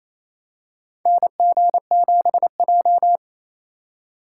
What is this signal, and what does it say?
Morse code, heard as a keyed tone, about 28 wpm, 700 Hz. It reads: NG7J
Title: USA Call Signs - 4 Characters - 4x Character Spacing - 28wpm